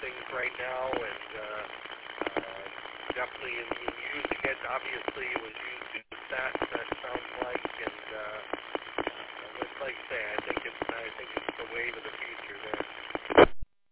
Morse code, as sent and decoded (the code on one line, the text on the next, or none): none